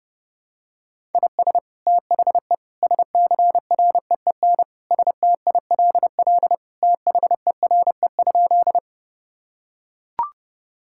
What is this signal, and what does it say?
Morse code, heard as a keyed tone, about 32 wpm, 700 Hz. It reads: IS THE SCREEN STILL THERE?
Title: Sentences from Top 1000 Words - Review Part 6 - 30wpm